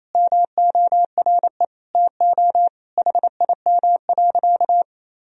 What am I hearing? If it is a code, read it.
Morse code, heard as a keyed tone, about 28 wpm, 700 Hz. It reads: MORE TO HIM.
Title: Sentences from Top 1000 Words - Review Part 1 - 28wpm